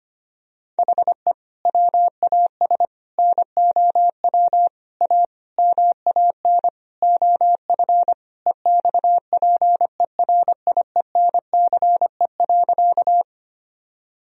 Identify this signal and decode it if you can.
Morse code, heard as a keyed tone, about 25 wpm, 700 Hz. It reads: HE WAS NOW A MAN OF EXPERIENCE.
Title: Sentences from Top 1000 Words - Review Part 3 - 25wpm